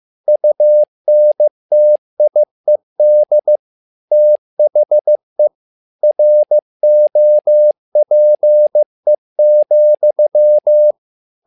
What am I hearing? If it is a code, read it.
Morse code, heard as a keyed tone, about 15 wpm, 600 Hz. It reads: UNTIED THE ROPE,
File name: learning center / tower trouble / tower trouble 15wpm